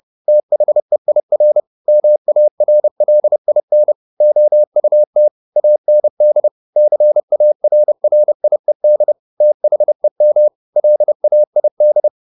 Morse code, heard as a keyed tone, about 30 wpm, 600 Hz. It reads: THEIR MARLIN OUT AND CARRIED THEM LAID
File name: the old man and the sea / the old man and the sea 1 30wpm